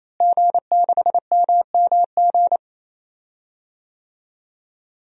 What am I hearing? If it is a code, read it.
Morse code, heard as a keyed tone, about 28 wpm, 700 Hz. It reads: G6MMG
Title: DXCC-100 Call Signs - 5 Characters - Repeated 3x - 28wpm